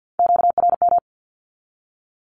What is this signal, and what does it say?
Morse code, heard as a keyed tone, about 50 wpm, 700 Hz. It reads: QRM
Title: Q-Codes with Spoken Meaning - 50wpm